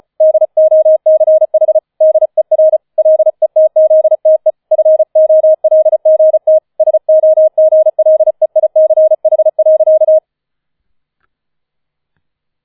Morse code, heard as a keyed tone, about 35 wpm, 630 Hz. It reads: DOCH DER LETZTE FOLGT SOGLEICH.